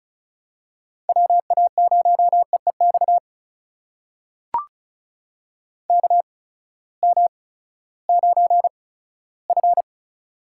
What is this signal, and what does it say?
Morse code, heard as a keyed tone, about 36 wpm, spaced as for 18 wpm, 700 Hz. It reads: WA0EEX KM9F
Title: USA Call Signs - 8x Character Spacing - 35wpm